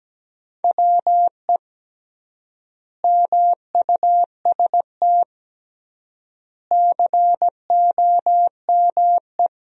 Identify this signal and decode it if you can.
Morse code, heard as a keyed tone, about 17 wpm, 700 Hz. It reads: WE MUST COME
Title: 3x Word-Spacing - Sentences from Top 200 Words - 17wpm